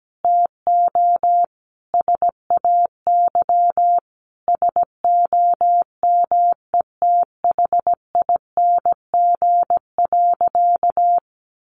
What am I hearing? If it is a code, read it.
Morse code, heard as a keyed tone, about 17 wpm, 700 Hz. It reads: TO SAY SOMETHING.
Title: Sentences from Top 1200 Words 17wpm